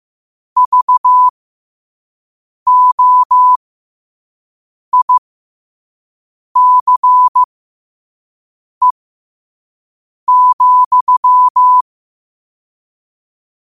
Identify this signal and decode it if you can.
Morse code, heard as a keyed tone, about 15 wpm, spaced as for 5 wpm, 1000 Hz. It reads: VOICE,
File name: 5wpm